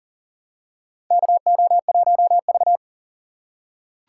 Morse code, heard as a keyed tone, about 40 wpm, 700 Hz. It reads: KO1V
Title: USA Call Signs - 3x Character Spacing - 40wpm